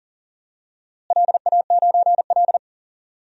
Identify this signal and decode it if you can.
Morse code, heard as a keyed tone, about 40 wpm, 700 Hz. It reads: LA9L
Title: DXCC-100 Call Signs - 4 Characters - Repeated 3x - 40wpm